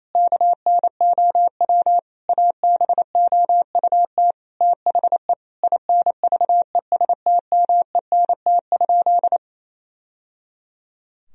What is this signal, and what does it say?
Morse code, heard as a keyed tone, about 28 wpm, 700 Hz. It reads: KNOW ABOUT THE INVESTMENT?
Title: Sentences from Top 1000 Words - Review Part 4 - 28wpm